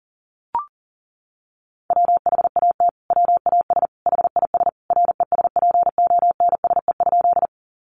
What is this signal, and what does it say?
Morse code, heard as a keyed tone, about 37 wpm, 700 Hz. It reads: WHAT WAS HIS RESPONSE?
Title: Sentences from Top 1400 Words 40wpm